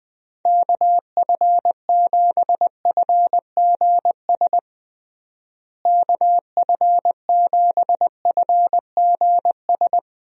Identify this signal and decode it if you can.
Morse code, heard as a keyed tone, about 20 wpm, 700 Hz. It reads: KF7FGS KF7FGS
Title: USA Call Signs - 6 Characters - Repeated 3x - 20wpm